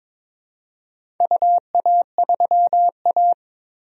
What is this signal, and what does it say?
Morse code, heard as a keyed tone, about 22 wpm, 700 Hz. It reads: UA3A
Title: DXCC-100 Call Signs - 4 Characters - 22wpm